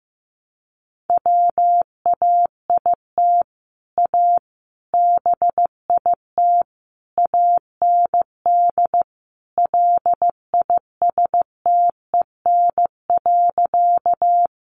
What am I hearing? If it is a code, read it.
Morse code, heard as a keyed tone, about 14 wpm, 700 Hz. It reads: WAIT A BIT AND LISTEN.
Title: Sentences from Top 900 Words 15wpm